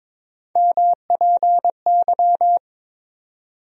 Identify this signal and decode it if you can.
Morse code, heard as a keyed tone, about 22 wpm, 700 Hz. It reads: MPY
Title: USA Call Sign Postfixes - 3 Letters 22wpm